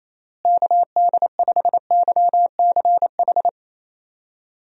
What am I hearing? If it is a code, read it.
Morse code, heard as a keyed tone, about 27 wpm, 700 Hz. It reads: KD5YCH
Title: USA Call Signs - 6 Characters - Repeated 3x - 28wpm